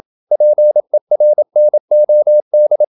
Morse code, heard as a keyed tone, about 27 wpm, 600 Hz. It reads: PERNOD
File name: Ferie QRQ 27wpm